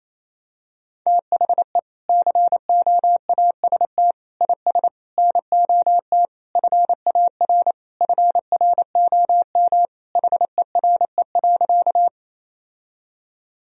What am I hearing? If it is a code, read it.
Morse code, heard as a keyed tone, about 28 wpm, 700 Hz. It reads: THE COAST IS NOT FAR FROM HERE.
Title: Sentences from Top 1000 Words - Review Part 4 - 28wpm